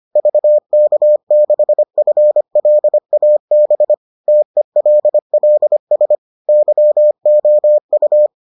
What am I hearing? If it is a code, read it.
Morse code, heard as a keyed tone, about 25 wpm, 600 Hz. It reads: VK6FLAB TELLS YOU